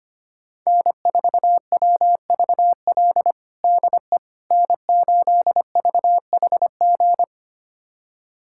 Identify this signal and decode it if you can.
Morse code, heard as a keyed tone, about 25 wpm, 700 Hz. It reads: N4WVL DE N8VHG
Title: USA Callsign DE Callsign - 5 Characters - 25wpm